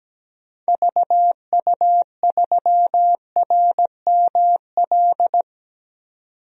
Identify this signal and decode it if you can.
Morse code, heard as a keyed tone, about 17 wpm, 700 Hz. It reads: VU3RML
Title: DXCC-100 Call Signs - 6 Characters - 17wpm